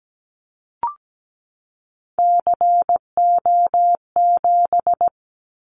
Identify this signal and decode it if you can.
Morse code, heard as a keyed tone, about 17 wpm, 700 Hz. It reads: CO7
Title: Callsign Prefixes for Top 100 DXCC - 17wpm